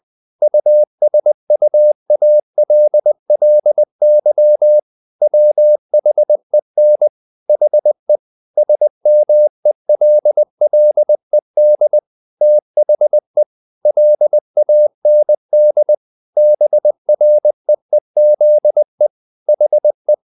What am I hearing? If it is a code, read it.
Morse code, heard as a keyed tone, about 20 wpm, 600 Hz. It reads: USUALLY WHEN HE SMELLED THE LAND BREEZE HE
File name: the old man and the sea / the old man and the sea part 1 20wpm